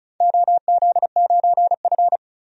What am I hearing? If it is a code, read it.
Morse code, heard as a keyed tone, about 36 wpm, 700 Hz. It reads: OZ9F
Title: DXCC-100 Call Signs - 4 Characters - 35wpm